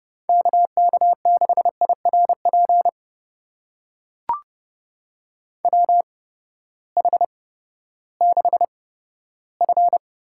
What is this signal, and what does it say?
Morse code, heard as a keyed tone, about 30 wpm, spaced as for 15 wpm, 700 Hz. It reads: KK6IRP WH6F
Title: USA Call Signs - 8x Character Spacing - 30wpm